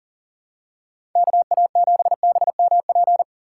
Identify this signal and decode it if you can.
Morse code, heard as a keyed tone, about 40 wpm, 700 Hz. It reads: KA7BMP
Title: USA Call Signs - 6 Characters - 3x Character Spacing - 40wpm